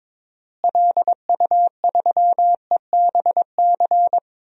Morse code, heard as a keyed tone, about 22 wpm, 700 Hz. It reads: LU3EBC
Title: DXCC-100 Call Signs - 6 Characters - Repeated 3x - 22wpm